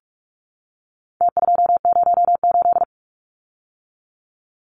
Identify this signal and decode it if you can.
Morse code, heard as a keyed tone, about 46 wpm, 700 Hz. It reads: T208
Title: Numbers - Contest Style - 4-Digit 45wpm